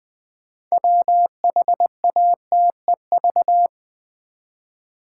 Morse code, heard as a keyed tone, about 20 wpm, 700 Hz. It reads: WHATEV
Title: Build A Word - 2 Letters At A Time - 8-Letter Words - 20wpm